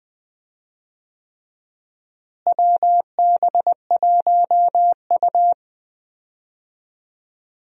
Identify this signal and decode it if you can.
Morse code, heard as a keyed tone, about 20 wpm, 690 Hz. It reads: WB1U
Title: USA Call Signs - 4 Characters - Repeated 3x - 20wpm